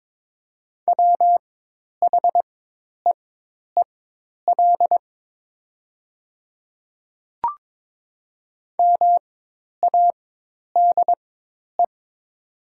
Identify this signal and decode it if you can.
Morse code, heard as a keyed tone, about 22 wpm, spaced as for 10 wpm, 700 Hz. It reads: WHEEL MADE
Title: 4x Character Spacing - Top 500 Words - 22wpm